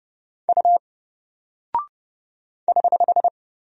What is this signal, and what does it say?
Morse code, heard as a keyed tone, about 30 wpm, 700 Hz. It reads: U <HH>
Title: Single Character (Letter:Number:Punctuation:Prosign) Alt 30wpm